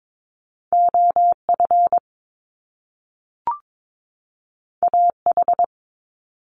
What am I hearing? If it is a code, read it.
Morse code, heard as a keyed tone, about 22 wpm, 700 Hz. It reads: OF AH
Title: Top Two Letter Combinations in Words 22wpm